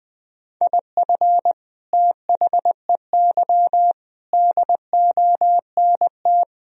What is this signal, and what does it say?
Morse code, heard as a keyed tone, about 20 wpm, 700 Hz. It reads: IF THEY DONT
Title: Sentences from Top 1000 Words - Review Part 1 - 20wpm